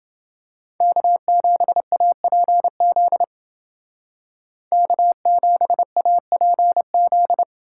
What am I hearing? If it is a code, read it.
Morse code, heard as a keyed tone, about 28 wpm, 700 Hz. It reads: K7APZ K7APZ